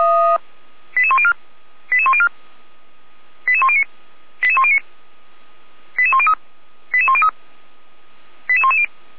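Five-tone selective call (ZVEI), heard as quick runs of five tones, tones of about 70 ms. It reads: 88104 88104 88109 88109 88103 88103 88110
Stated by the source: ZVEI five-tone call received by SDR.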